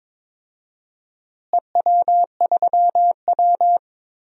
Morse code, heard as a keyed tone, about 22 wpm, 700 Hz. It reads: EW3W
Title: DXCC-100 Call Signs - 4 Characters - Repeated 2x - 22wpm